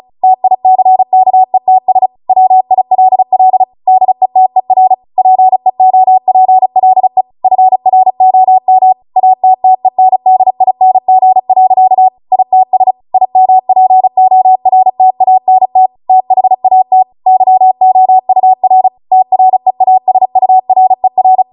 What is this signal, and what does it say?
Morse code, heard as a keyed tone, about 35 wpm, 750 Hz. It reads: TICKETS WILL DETER PEOPLE FROM ATTENDING. ITS IMPORTANT THAT YOUR TREASURER
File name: Morse / 35WPM